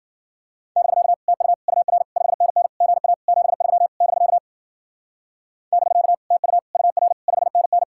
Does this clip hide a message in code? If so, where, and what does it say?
Yes: Morse code from 0.8 to 7.8 s, heard as a keyed tone, about 52 wpm, 700 Hz. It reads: <BK> TU UR 5NN GA 73 <BK> <BK> TU UR 5NN